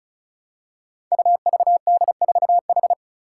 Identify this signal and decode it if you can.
Morse code, heard as a keyed tone, about 36 wpm, 700 Hz. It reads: UVD4H
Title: Five Challenging Letter-Number Combinations 35wpm